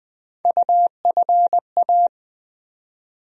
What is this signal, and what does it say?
Morse code, heard as a keyed tone, about 20 wpm, 700 Hz. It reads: UFA
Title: USA Call Sign Postfixes - 3 Letters 20wpm